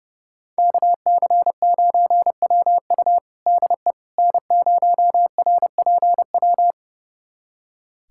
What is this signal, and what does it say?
Morse code, heard as a keyed tone, about 31 wpm, 700 Hz. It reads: KC9WU DE N0RPW
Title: USA Callsign DE Callsign - 5 Characters - 30wpm